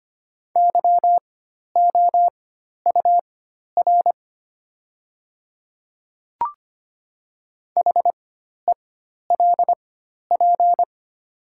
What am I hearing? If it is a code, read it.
Morse code, heard as a keyed tone, about 25 wpm, spaced as for 12 wpm, 700 Hz. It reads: YOUR HELP
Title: 4x Character Spacing - Top 500 Words - 25wpm